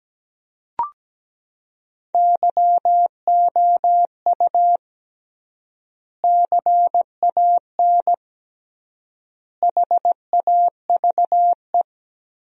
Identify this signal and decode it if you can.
Morse code, heard as a keyed tone, about 17 wpm, 700 Hz. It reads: YOU CAN HAVE
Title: 3x Word-Spacing - Sentences from Top 100 Words - 17wpm